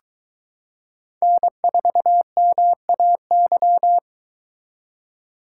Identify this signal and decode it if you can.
Morse code, heard as a keyed tone, about 22 wpm, 700 Hz. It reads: N4MAY